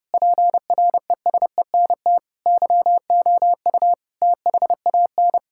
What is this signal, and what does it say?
Morse code, heard as a keyed tone, about 28 wpm, 700 Hz. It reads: PRESENT YOU THAN
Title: Sentences from Top 1000 Words - Review Part 6 - 30wpm